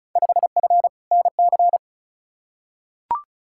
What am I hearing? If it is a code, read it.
Morse code, heard as a keyed tone, about 35 wpm, 700 Hz. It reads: 5F NC
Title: Field Day Exchanges Encore - 35wpm